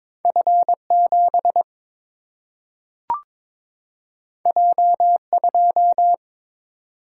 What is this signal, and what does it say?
Morse code, heard as a keyed tone, about 22 wpm, 700 Hz. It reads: F 7 J 2
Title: Single-Letter Single-Number 22wpm